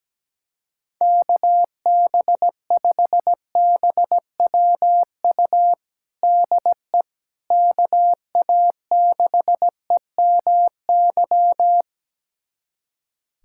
Morse code, heard as a keyed tone, about 17 wpm, 700 Hz. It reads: KB5BWU DE KA6EMY
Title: USA Callsign DE Callsign - 6 Characters - 17wpm